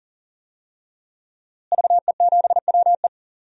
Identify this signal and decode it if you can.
Morse code, heard as a keyed tone, about 37 wpm, 700 Hz. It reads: VE7WE
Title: DXCC-100 Call Signs - 5 Characters - Repeated 3x - 40wpm